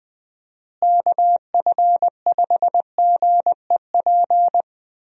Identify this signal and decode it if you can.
Morse code, heard as a keyed tone, about 21 wpm, 700 Hz. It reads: KF5GEP
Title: USA Call Signs - 6 Characters - Repeated 2x - 20wpm